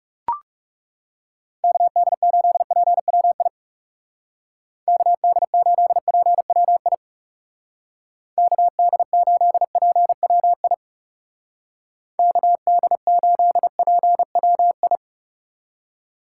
Morse code, heard as a keyed tone, about 38 wpm, 700 Hz. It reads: KD8PWI KD8PWI KD8PWI KD8PWI